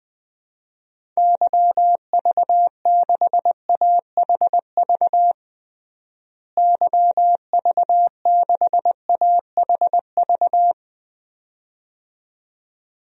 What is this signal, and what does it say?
Morse code, heard as a keyed tone, about 20 wpm, 700 Hz. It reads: YV6AHV YV6AHV
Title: DXCC-100 Call Signs - 6 Characters - Repeated 2x - 20wpm